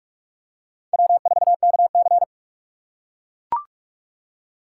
Morse code, heard as a keyed tone, about 42 wpm, 700 Hz. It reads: W4KC